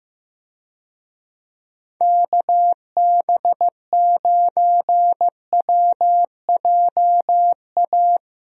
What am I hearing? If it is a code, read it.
Morse code, heard as a keyed tone, about 15 wpm, 700 Hz. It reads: KB9WJA
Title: USA Call Signs - 6 Characters - Repeated 3x - 15wpm